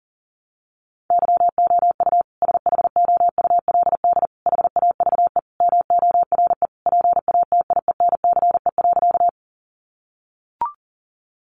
Morse code, heard as a keyed tone, about 40 wpm, 700 Hz. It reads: YOU SHOULD HAVE MORE PATIENCE.
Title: Sentences from Top 1300 Words 40wpm